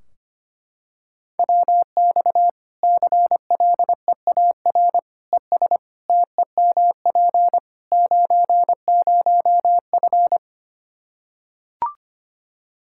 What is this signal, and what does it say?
Morse code, heard as a keyed tone, about 25 wpm, 700 Hz. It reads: WX CLEAR ES TEMP 90F
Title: Practice QSO - Exchange 2 - Info - 25wpm